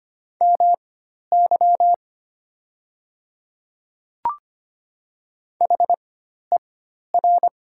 Morse code, heard as a keyed tone, about 25 wpm, spaced as for 12 wpm, 700 Hz. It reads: MY HER
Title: 4x Character Spacing - Top 300 Words - 25wpm